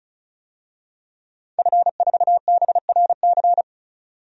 Morse code, heard as a keyed tone, about 36 wpm, 700 Hz. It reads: F4BRC
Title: DXCC-100 Call Signs - 5 Characters - Repeated 2x - 35wpm